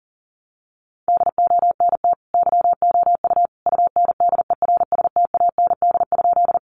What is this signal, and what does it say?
Morse code, heard as a keyed tone, about 40 wpm, 700 Hz. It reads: DONT YOU UNDERSTAND?
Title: Sentences from Top 600 Words 40wpm